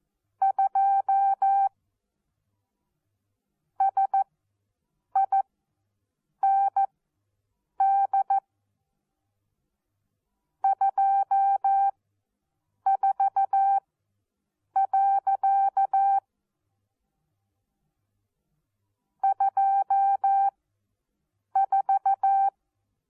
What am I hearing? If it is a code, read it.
Morse code, heard as a keyed tone, about 14 wpm, spaced as for 7 wpm, 780 Hz. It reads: 2 SIND 24. 24
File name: Onlinelehrgang / morsen / DJ4UF Morsen14